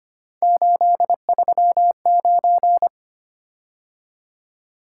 Morse code, heard as a keyed tone, about 25 wpm, 700 Hz. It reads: 839
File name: Three Letter-Number Combinations - 25wpm